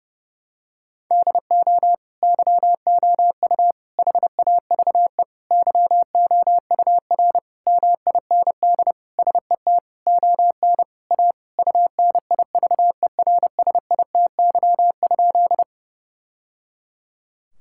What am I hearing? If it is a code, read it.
Morse code, heard as a keyed tone, about 30 wpm, 700 Hz. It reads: DO YOU HAVE YOUR MIND SET ON A UNIVERSITY?
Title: Sentences from Top 1000 Words - Review Part 5 - 30wpm